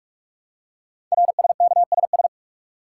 Morse code, heard as a keyed tone, about 44 wpm, 700 Hz. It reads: RSKSS
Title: Code Groups - 5 Letters - 45wpm